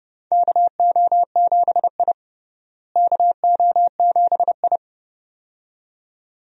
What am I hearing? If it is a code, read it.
Morse code, heard as a keyed tone, about 30 wpm, 700 Hz. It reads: KO7I KO7I
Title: USA Call Signs - 4 Characters - Repeated 2x - 30wpm